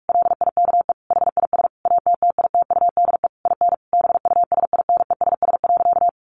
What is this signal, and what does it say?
Morse code, heard as a keyed tone, about 44 wpm, 700 Hz. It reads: LIKE HIS ATTITUDE IN BUSINESS.
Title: Sentences from Top 1300 Words 45wpm